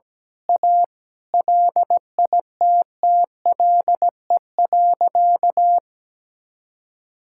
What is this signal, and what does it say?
Morse code, heard as a keyed tone, about 17 wpm, 700 Hz. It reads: A LITTLE.
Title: Sentences from Top 1000 Words - Review Part 5 - 17wpm